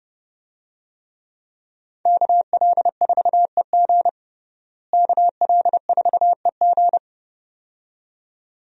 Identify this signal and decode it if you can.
Morse code, heard as a keyed tone, about 32 wpm, 700 Hz. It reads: KL4EG KL4EG
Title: USA Call Signs - 5 Characters - Repeated 2x - 30wpm